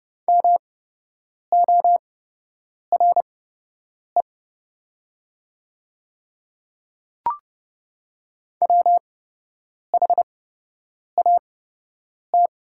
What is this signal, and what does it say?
Morse code, heard as a keyed tone, about 30 wpm, spaced as for 8 wpm, 700 Hz. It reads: MORE WHAT
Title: 8x Character Spacing - Top 300 Words - 30wpm